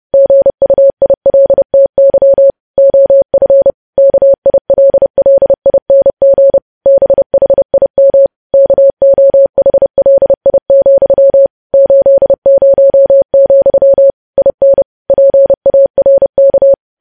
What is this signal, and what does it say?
Morse code, heard as a keyed tone, about 30 wpm, 550 Hz. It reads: GUILTY OF KILLING BHIM KOHLI, 80, IN PARK
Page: news headlines in morse code at 30 wpm